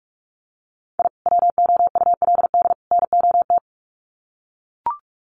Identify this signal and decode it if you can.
Morse code, heard as a keyed tone, about 44 wpm, 700 Hz. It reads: I WOULD NOT
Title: Sets of 3 Words 45wpm